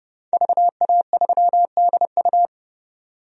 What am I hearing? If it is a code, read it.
Morse code, heard as a keyed tone, about 30 wpm, 700 Hz. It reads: VA3DU
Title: DXCC-100 Call Signs - 5 Characters - 30wpm